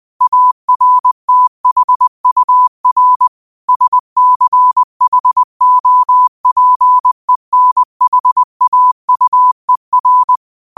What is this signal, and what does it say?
Morse code, heard as a keyed tone, about 20 wpm, 1000 Hz. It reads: ARTHUR SCHOPENHAUER
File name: qotdfiles / 20wpm